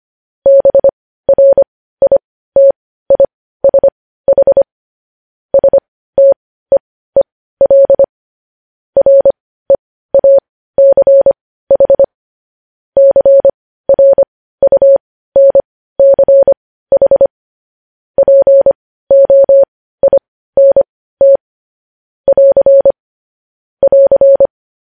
Morse code, heard as a keyed tone, about 25 wpm, spaced as for 15 wpm, 550 Hz. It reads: BRITISH STEEL REACH CRUNCH POINT <AR> <AR>
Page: news-headlines-in-morse-code-at-15-wpm